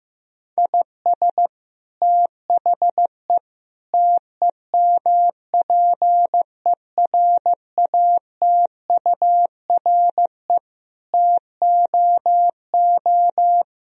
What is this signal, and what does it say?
Morse code, heard as a keyed tone, about 15 wpm, 700 Hz. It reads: IS THE TEMPERATURE TOO
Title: Sentences from Top 1000 Words - Review Part 4 - 15wpm